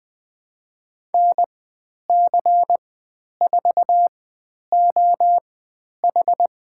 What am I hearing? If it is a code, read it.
Morse code, heard as a keyed tone, about 20 wpm, spaced as for 10 wpm, 700 Hz. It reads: NC4OH
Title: 102 Lesson 17g - Letter u - Call Signs 20wpm